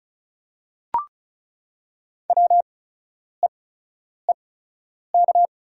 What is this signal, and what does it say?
Morse code, heard as a keyed tone, about 36 wpm, spaced as for 10 wpm, 700 Hz. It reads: WEEK
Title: 8x Character Spacing - Top 500 Words - 35wpm